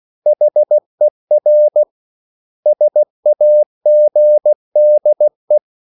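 Morse code, heard as a keyed tone, about 16 wpm, 600 Hz. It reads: HER SAGDE
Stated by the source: Slow (QRS) Morse code at 16 wpm, from CW med Gnister.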